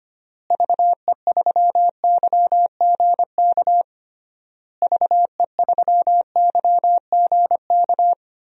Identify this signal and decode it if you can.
Morse code, heard as a keyed tone, about 25 wpm, 700 Hz. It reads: VE3YGK VE3YGK
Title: DXCC-100 Call Signs - 6 Characters - Repeated 3x - 25wpm